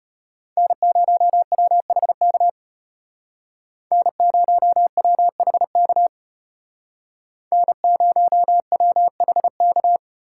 Morse code, heard as a keyed tone, about 32 wpm, 700 Hz. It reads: N0WHK N0WHK N0WHK